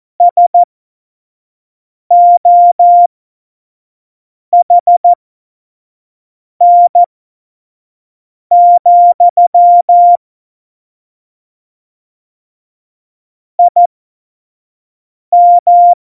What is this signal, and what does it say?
Morse code, heard as a keyed tone, about 14 wpm, spaced as for 5 wpm, 700 Hz. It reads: SOHN, IM
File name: Morsekurs / Witze / Witz 29. Dezember 2020, 25 F70